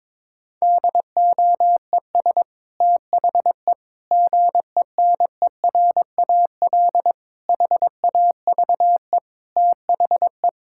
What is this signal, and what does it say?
Morse code, heard as a keyed tone, about 22 wpm, 700 Hz. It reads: DOES THE GENERAL HAVE THE
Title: Sentences from Top 1000 Words - Review Part 6 - 22wpm